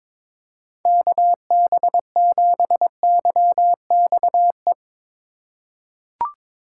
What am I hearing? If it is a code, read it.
Morse code, heard as a keyed tone, about 22 wpm, 700 Hz. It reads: KB7YXE
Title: USA Call Signs - 6 Characters - 4x Character Spacing - 22wpm